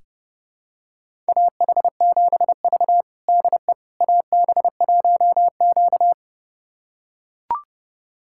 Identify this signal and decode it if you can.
Morse code, heard as a keyed tone, about 32 wpm, 700 Hz. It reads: AH7V DE AB1Q
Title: USA Callsign DE Callsign - 4 Characters - 30wpm